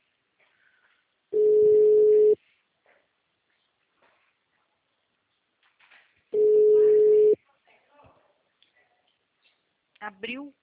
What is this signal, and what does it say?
Ringback tone (Europe, 425 Hz), with speech.
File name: out- -2016- - - .0